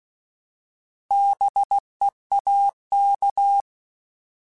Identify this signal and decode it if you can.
Morse code, heard as a keyed tone, about 16 wpm, 800 Hz. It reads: BEAK
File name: Alphabetical / Alphabetical mp3s / Bs @15